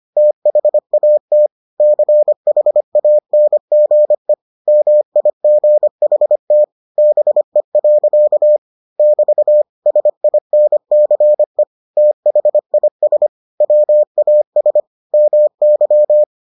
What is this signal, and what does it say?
Morse code, heard as a keyed tone, about 25 wpm, 600 Hz. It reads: THAT CHANGE MIGHT BE. <BT> SINCE THIS WAS MY